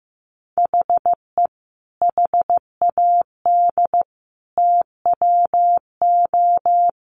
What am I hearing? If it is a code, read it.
Morse code, heard as a keyed tone, about 15 wpm, 700 Hz. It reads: HE HAD TWO
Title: Sentences from Top 300 Words 15wpm